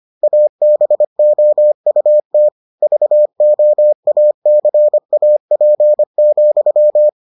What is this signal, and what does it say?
Morse code, heard as a keyed tone, about 25 wpm, 600 Hz. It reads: ABOUT VOACAP,